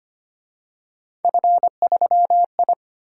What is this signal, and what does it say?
Morse code, heard as a keyed tone, about 29 wpm, 700 Hz. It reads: F3I
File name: Speed-Racing - Three Letter-Number Combinations - 25wpm